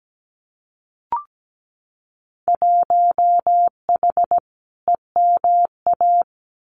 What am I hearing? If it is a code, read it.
Morse code, heard as a keyed tone, about 17 wpm, 700 Hz. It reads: 1H EMA
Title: Winter Field Day Exchanges 17wpm